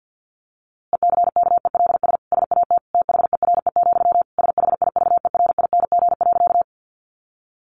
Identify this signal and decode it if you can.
Morse code, heard as a keyed tone, about 50 wpm, 700 Hz. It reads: ECKELS SAT THERE, SHIVERING.
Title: A Sound Of Thunder 50wpm